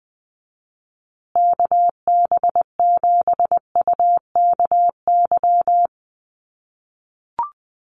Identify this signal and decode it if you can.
Morse code, heard as a keyed tone, about 20 wpm, 700 Hz. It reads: KB7UKY
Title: USA Call Signs Repeated 2x 20wpm